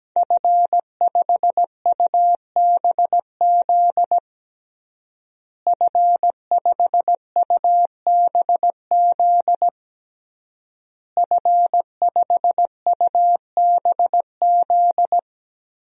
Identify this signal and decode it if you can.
Morse code, heard as a keyed tone, about 17 wpm, 700 Hz. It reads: F5UBZ F5UBZ F5UBZ
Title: DXCC-100 Call Signs - 5 Characters - Repeated 3x - 17wpm